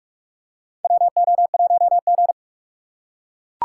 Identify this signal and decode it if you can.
Morse code, heard as a keyed tone, about 45 wpm, 700 Hz. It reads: WO1G
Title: USA Call Signs - 4 Characters - 3x Character Spacing - 45wpm